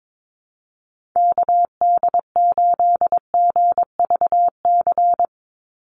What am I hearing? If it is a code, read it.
Morse code, heard as a keyed tone, about 22 wpm, 700 Hz. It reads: KD8GVC
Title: USA Call Signs 22wpm